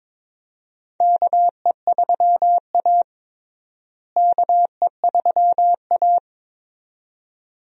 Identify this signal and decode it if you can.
Morse code, heard as a keyed tone, about 22 wpm, 700 Hz. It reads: KE3A KE3A
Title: USA Call Signs - 4 Characters - Repeated 2x - 22wpm